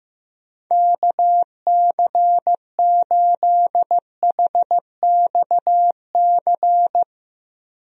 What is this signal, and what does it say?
Morse code, heard as a keyed tone, about 16 wpm, 700 Hz. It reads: KC8HXC
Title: USA Call Signs - 6 Characters - Repeated 2x - 15wpm